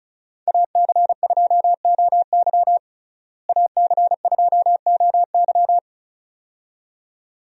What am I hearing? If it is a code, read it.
Morse code, heard as a keyed tone, about 36 wpm, 700 Hz. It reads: AC2OY AC2OY
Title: USA Call Signs - 5 Characters - Repeated 2x - 35wpm